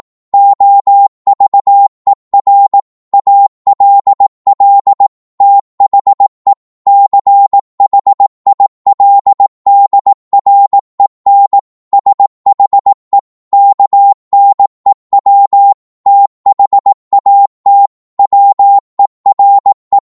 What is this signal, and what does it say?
Morse code, heard as a keyed tone, about 18 wpm, 800 Hz. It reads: OVER ALL THE CHILDREN SHE KNEW THAT WERE
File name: Morse / psg-18wpm